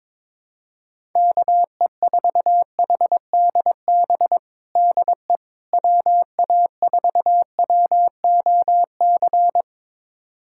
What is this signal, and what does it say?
Morse code, heard as a keyed tone, about 22 wpm, 700 Hz. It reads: KE4HDB DE WA4WOC
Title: USA Callsign DE Callsign - 6 Characters - 22wpm